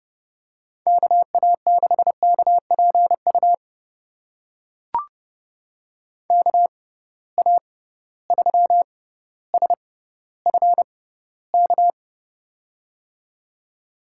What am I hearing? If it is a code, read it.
Morse code, heard as a keyed tone, about 30 wpm, spaced as for 15 wpm, 700 Hz. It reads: KA6KPU KA3SFK
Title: USA Call Signs - 6 Characters - 6x Character Spacing - 30wpm